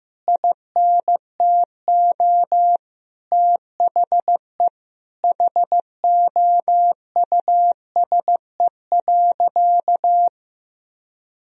Morse code, heard as a keyed tone, about 15 wpm, 700 Hz. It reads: INTO THE HOUSE.
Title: Sentences from Top 1000 Words - Review Part 1 - 15wpm